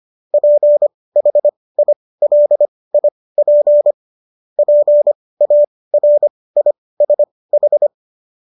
Morse code, heard as a keyed tone, about 25 wpm, spaced as for 18 wpm, 590 Hz. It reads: PHILIP PARISH
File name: qso217 18